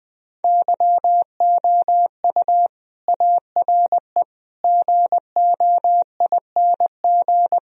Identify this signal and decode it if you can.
Morse code, heard as a keyed tone, about 20 wpm, 700 Hz. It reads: YOU ARE GOING